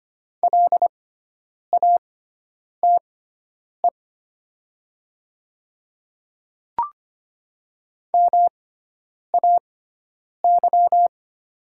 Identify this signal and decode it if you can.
Morse code, heard as a keyed tone, about 25 wpm, spaced as for 9 wpm, 700 Hz. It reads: LATE MAY
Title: 6x Character Spacing - Top 300 Words - 25wpm